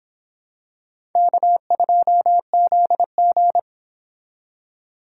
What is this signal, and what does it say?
Morse code, heard as a keyed tone, about 26 wpm, 700 Hz. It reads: K2ZG